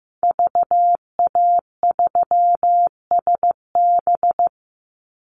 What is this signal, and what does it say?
Morse code, heard as a keyed tone, about 15 wpm, 700 Hz. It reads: VA3SB
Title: Call Signs Repeated 2x - 15wpm